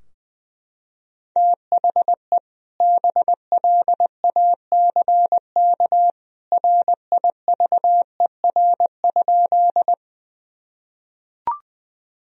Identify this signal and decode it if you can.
Morse code, heard as a keyed tone, about 21 wpm, 700 Hz. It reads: THE BLACK RIVER?
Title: Sentences from Top 1000 Words - Review Part 2 - 20wpm